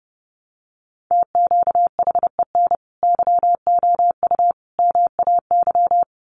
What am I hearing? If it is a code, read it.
Morse code, heard as a keyed tone, about 30 wpm, 700 Hz. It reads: TQHEN YOU MAY
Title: Sentences from Top 100 Words 30wpm